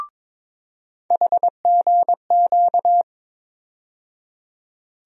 Morse code, heard as a keyed tone, about 22 wpm, 700 Hz. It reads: HGQ
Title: USA Call Sign Postfixes - 3 Letters 22wpm